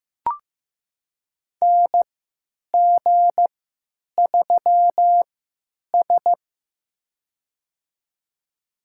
Morse code, heard as a keyed tone, about 15 wpm, spaced as for 9 wpm, 700 Hz. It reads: NG3S
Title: USA Call Signs - 4 Characters - 3x Character Spacing - 15wpm